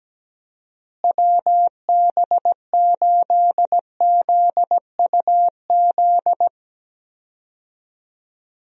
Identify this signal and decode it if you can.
Morse code, heard as a keyed tone, about 17 wpm, 700 Hz. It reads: WB8ZUZ
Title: USA Call Signs - 6 Characters - Repeated 2x - 17wpm